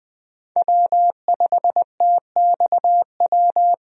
Morse code, heard as a keyed tone, about 21 wpm, 700 Hz. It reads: W5TXW
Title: USA Call Signs - 5 Characters - 20wpm